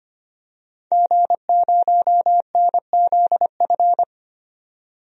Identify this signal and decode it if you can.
Morse code, heard as a keyed tone, about 25 wpm, 700 Hz. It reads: G0NZF
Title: DXCC-100 Call Signs - 5 Characters - 25wpm